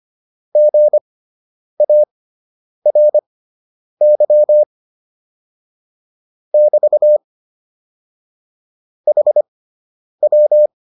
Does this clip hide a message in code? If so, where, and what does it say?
Yes: Morse code from 0.6 to 10.7 s, heard as a keyed tone, about 25 wpm, spaced as for 9 wpm, 600 Hz. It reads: GARY <BT> HW